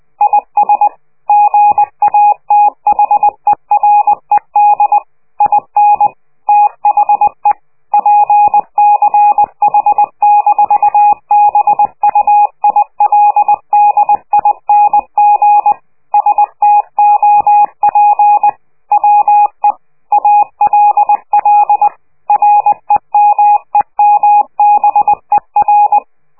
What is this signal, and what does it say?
Morse code, heard as a keyed tone, about 20 wpm, 830 Hz. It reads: IS GATHERED IN THE PCH-BUILDING STOP WE ALL REMEMBER
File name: worldcoastal / scheveningenradio / morsepch